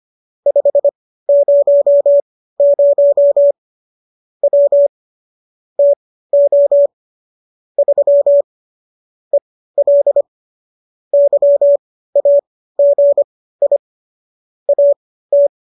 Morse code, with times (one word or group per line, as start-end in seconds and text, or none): 0.5-3.5 500
4.4-4.9 W
5.8-6.9 TO
7.8-8.4 3
9.3-10.2 EL
11.1-13.8 YAGI
14.7-15.5 AT